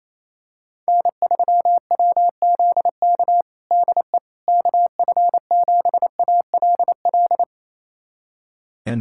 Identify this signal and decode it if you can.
Morse code, heard as a keyed tone, about 28 wpm, 700 Hz. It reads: N3WZK DE KF7ALL
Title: USA Callsign DE Callsign 28wpm